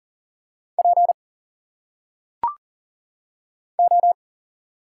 Morse code, heard as a keyed tone, about 40 wpm, 700 Hz. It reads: P O
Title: Single Letters - Phonetic Pronunciation 40wpm